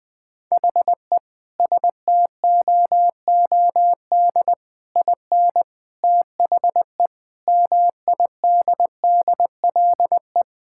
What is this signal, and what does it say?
Morse code, heard as a keyed tone, about 21 wpm, 700 Hz. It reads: HE STOOD IN THE MIDDLE
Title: Sentences from Top 1000 Words - Review Part 6 - 20wpm